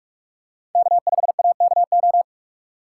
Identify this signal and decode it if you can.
Morse code, heard as a keyed tone, about 46 wpm, 700 Hz. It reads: K5AKO
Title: USA Call Signs - 3x Character Spacing - 45wpm